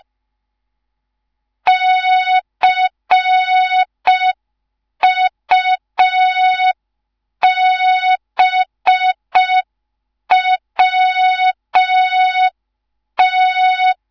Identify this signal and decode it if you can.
Morse code, heard as a keyed tone, about 5 wpm, 3700 Hz. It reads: CUBWT
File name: Morze-HA2MN / hang / 10GWZ HA2MN